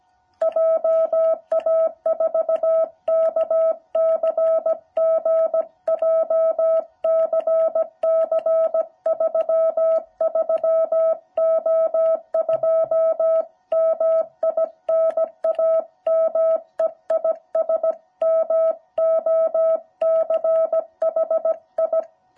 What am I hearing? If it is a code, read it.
Morse code, heard as a keyed tone, about 17 wpm, 650 Hz. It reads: JA4KCG JCC 33O2 MI NAME IS MOCHI